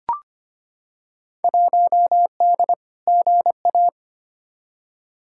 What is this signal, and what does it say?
Morse code, heard as a keyed tone, about 25 wpm, 700 Hz. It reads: 1D GA
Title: Field Day Exchanges - 25wpm